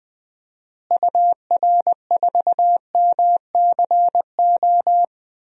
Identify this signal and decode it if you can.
Morse code, heard as a keyed tone, about 20 wpm, 700 Hz. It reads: UR4MCO
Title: DXCC-100 Call Signs - 6 Characters - Repeated 2x - 20wpm